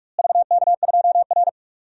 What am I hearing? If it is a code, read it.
Morse code, heard as a keyed tone, about 41 wpm, 700 Hz. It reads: VK2R
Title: DXCC-100 Call Signs - 4 Characters - Repeated 3x - 45wpm